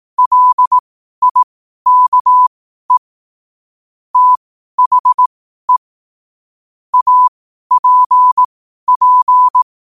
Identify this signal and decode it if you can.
Morse code, heard as a keyed tone, about 18 wpm, spaced as for 12 wpm, 1000 Hz. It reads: LIKE THE APP
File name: qotdfiles / 12wpm